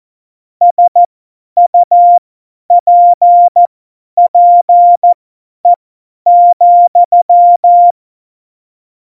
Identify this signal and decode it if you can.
Morse code, heard as a keyed tone, about 14 wpm, spaced as for 10 wpm, 700 Hz. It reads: SUPPE,
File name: Morsekurs / Witze / Witz 09. April 2019, 50 F70